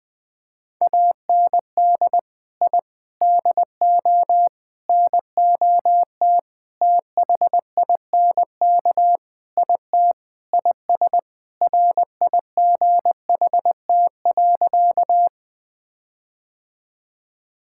Morse code, heard as a keyed tone, about 20 wpm, 700 Hz. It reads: AND I DO NOT THINK IT IS RIGHT.
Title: Sentences with 5-Word Phrases - Spoken Beginning - 20wpm